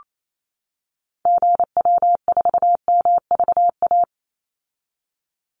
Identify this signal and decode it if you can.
Morse code, heard as a keyed tone, about 28 wpm, 700 Hz. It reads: GW4MVA
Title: Call Signs - 28wpm